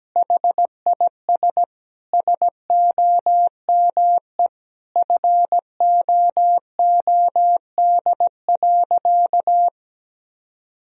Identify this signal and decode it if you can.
Morse code, heard as a keyed tone, about 17 wpm, 700 Hz. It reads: HIS SOME FOOD.
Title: Sentences from Top 1000 Words - Review Part 1 - 17wpm